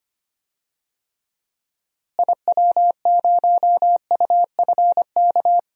Morse code, heard as a keyed tone, about 25 wpm, 700 Hz. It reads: IW0UFK
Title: DXCC-100 Call Signs - 6 Characters - Repeated 2x - 25wpm